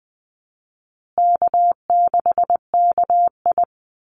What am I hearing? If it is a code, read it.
Morse code, heard as a keyed tone, about 20 wpm, 700 Hz. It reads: K6KI
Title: Call Signs Repeated 2x - 20wpm